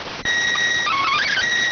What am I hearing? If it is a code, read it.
SSTV (slow-scan television), PD 290.